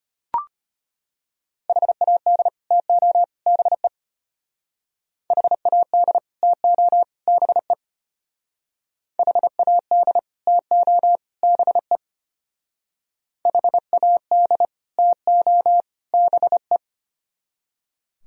Morse code, heard as a keyed tone, about 33 wpm, 700 Hz. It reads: HAD TO BE HAD TO BE HAD TO BE HAD TO BE